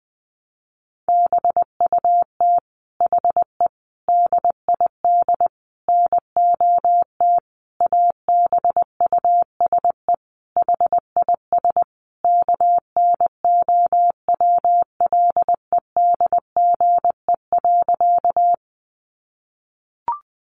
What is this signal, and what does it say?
Morse code, heard as a keyed tone, about 20 wpm, 700 Hz. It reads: BUT HE DID NOT ABUSE HIS KNOWLEDGE.
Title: Sentences from Top 1300 Words 20wpm